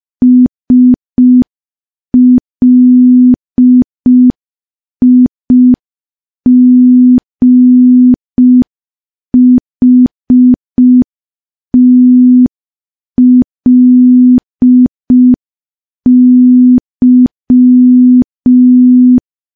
Morse code, heard as a keyed tone, about 5 wpm, 250 Hz. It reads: SLIGHTLY
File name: morse / morse-training-05WPM